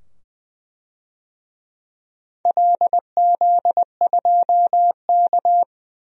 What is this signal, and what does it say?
Morse code, heard as a keyed tone, about 20 wpm, 700 Hz. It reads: LZ2K
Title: DXCC-100 Call Signs - 4 Characters - 20wpm